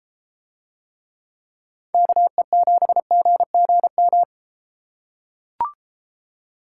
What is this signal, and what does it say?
Morse code, heard as a keyed tone, about 32 wpm, 700 Hz. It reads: KE7GGM